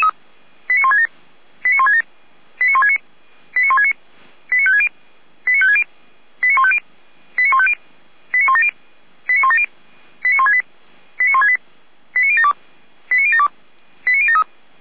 Five-tone selective call (ZVEI), heard as quick runs of five tones, tones of about 70 ms. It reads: 89167 89167 89169 89169 89560 89560 89150 89150 89170 89170 89168 89168 89072 89072 89073